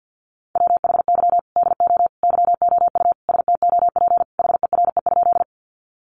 Morse code, heard as a keyed tone, about 48 wpm, 700 Hz. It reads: WHY DO YOU STOP HERE?
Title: Sentences from Top 300 Words 50wpm